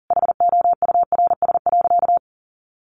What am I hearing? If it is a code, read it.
Morse code, heard as a keyed tone, about 40 wpm, 700 Hz. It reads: HOURS.
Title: Additional Easy Phrases 40wpm